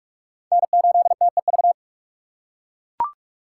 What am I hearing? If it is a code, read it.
Morse code, heard as a keyed tone, about 42 wpm, 700 Hz. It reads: N8TEV
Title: USA Call Signs - 5 Characters - 6x Character Spacing - 45wpm